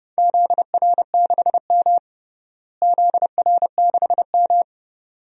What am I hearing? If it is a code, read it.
Morse code, heard as a keyed tone, about 30 wpm, 700 Hz. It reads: ZR6M ZR6M
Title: DXCC-100 Call Signs - 4 Characters - Repeated 2x - 30wpm